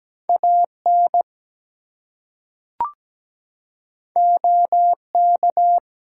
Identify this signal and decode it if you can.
Morse code, heard as a keyed tone, about 17 wpm, 700 Hz. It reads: AN OK
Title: Two Letter Words - 17wpm